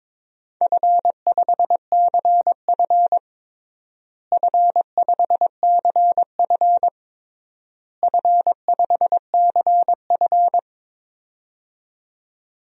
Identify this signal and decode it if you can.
Morse code, heard as a keyed tone, about 22 wpm, 700 Hz. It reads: F5CF F5CF F5CF
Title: DXCC-100 Call Signs - 4 Characters - Repeated 3x - 22wpm